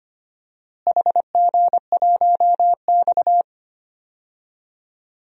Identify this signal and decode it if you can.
Morse code, heard as a keyed tone, about 25 wpm, 700 Hz. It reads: HG1X
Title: DXCC-100 Call Signs - 4 Characters - Repeated 2x - 25wpm